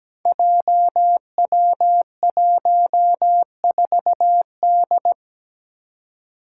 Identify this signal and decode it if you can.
Morse code, heard as a keyed tone, about 17 wpm, 700 Hz. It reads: JW14D
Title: Five Challenging Letter-Number Combinations 17wpm